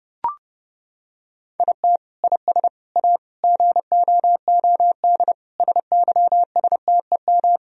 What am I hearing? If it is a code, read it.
Morse code, heard as a keyed tone, about 32 wpm, 700 Hz. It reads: IT IS A GOOD SYSTEM
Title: Sentences from Top 1000 Words - Review Part 3 - 30wpm